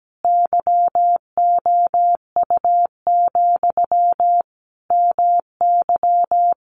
Morse code, heard as a keyed tone, about 17 wpm, 700 Hz. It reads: YOU, MY
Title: Sentences from Top 600 Words 17wpm